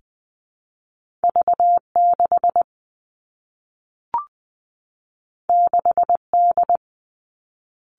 Morse code, heard as a keyed tone, about 20 wpm, 700 Hz. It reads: V 6 6 D
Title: Two Challenging Letter-Number Combinations 20wpm